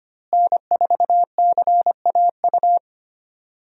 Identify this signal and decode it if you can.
Morse code, heard as a keyed tone, about 25 wpm, 700 Hz. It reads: N4CAU
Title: USA Call Signs - 5 Characters - Repeated 2x - 25wpm